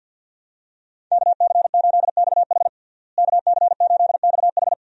Morse code, heard as a keyed tone, about 51 wpm, 700 Hz. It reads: KC8XH KC8XH
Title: USA Call Signs - 5 Characters - Repeated 3x - 50wpm